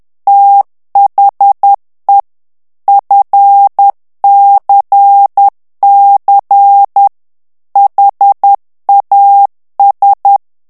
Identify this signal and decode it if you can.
Morse code, heard as a keyed tone, about 10 wpm, 800 Hz. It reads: THE FCC HAS
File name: ANGLAIS texte clair 3 vitesse 10 mots minute